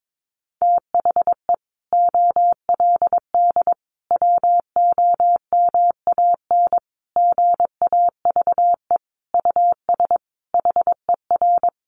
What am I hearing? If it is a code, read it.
Morse code, heard as a keyed tone, about 22 wpm, 700 Hz. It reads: THE OLD WOMAN GAVE US HER
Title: Sentences from Top 1100 Words 22wpm